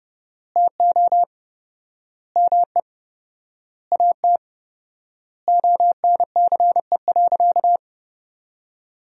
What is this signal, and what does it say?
Morse code, heard as a keyed tone, about 30 wpm, 700 Hz. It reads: TO ME AT ONCE.
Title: 4x Word-Spacing - Sentences from Top 500 Words - 30wpm